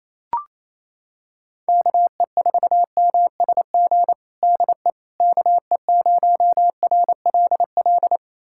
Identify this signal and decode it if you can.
Morse code, heard as a keyed tone, about 28 wpm, 700 Hz. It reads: KE4MSG DE KE0RLL
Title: USA Callsign DE Callsign - 6 Characters - 28wpm